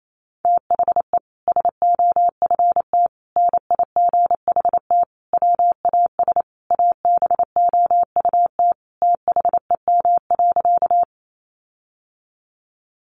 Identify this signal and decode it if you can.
Morse code, heard as a keyed tone, about 28 wpm, 700 Hz. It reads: THE SOFT NIGHT WAS ABOUT THEM.
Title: Sentences from Top 1500 Words 28wpm